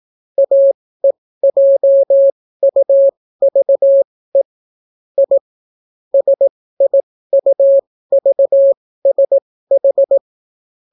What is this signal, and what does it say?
Morse code, heard as a keyed tone, about 18 wpm, spaced as for 14 wpm, 550 Hz. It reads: AEJUVE I SIUVSH